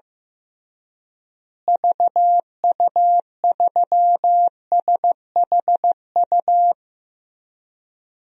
Morse code, heard as a keyed tone, about 15 wpm, 700 Hz. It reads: VU3SHU
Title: DXCC-100 Call Signs - 6 Characters - Repeated 2x - 15wpm